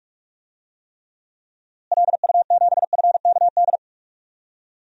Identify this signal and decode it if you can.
Morse code, heard as a keyed tone, about 47 wpm, 700 Hz. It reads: LU7FKD
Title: DXCC-100 Call Signs - 6 Characters - Repeated 2x - 45wpm